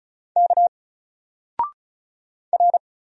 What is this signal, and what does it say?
Morse code, heard as a keyed tone, about 35 wpm, 700 Hz. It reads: K R
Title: Single Letter-Number Alt 35wpm